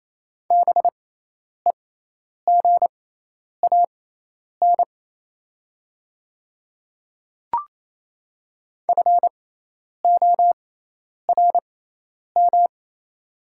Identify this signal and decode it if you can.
Morse code, heard as a keyed tone, about 28 wpm, spaced as for 10 wpm, 700 Hz. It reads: BEGAN FORM
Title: 6x Character Spacing - Top 300 Words - 28wpm